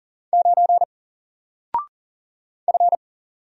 Morse code, heard as a keyed tone, about 38 wpm, 700 Hz. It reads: 9 F
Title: Single Letter-Number Alt 40wpm